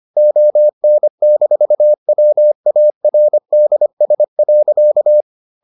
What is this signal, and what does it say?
Morse code, heard as a keyed tone, about 25 wpm, 600 Hz. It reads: ON-WARDS.